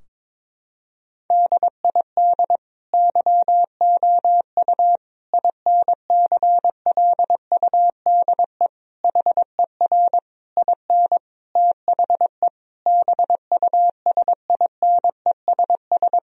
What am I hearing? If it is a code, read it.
Morse code, heard as a keyed tone, about 22 wpm, 700 Hz. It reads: DID YOU INCLUDE HER IN THE BUSINESS
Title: Sentences from Top 1000 Words - Review Part 3 - 22wpm